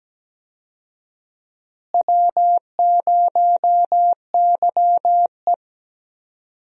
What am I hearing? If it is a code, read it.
Morse code, heard as a keyed tone, about 17 wpm, 700 Hz. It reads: W0YE
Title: USA Call Signs - 4 Characters - Repeated 3x - 17wpm